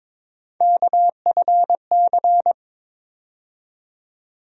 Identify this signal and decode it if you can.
Morse code, heard as a keyed tone, about 22 wpm, 700 Hz. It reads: KFC
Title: USA Call Sign Postfixes - 3 Letters 22wpm